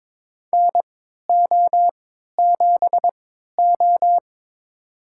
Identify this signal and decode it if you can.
Morse code, heard as a keyed tone, about 22 wpm, spaced as for 13 wpm, 700 Hz. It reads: NO7O
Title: USA Call Signs - 4 Characters - 3x Character Spacing - 22wpm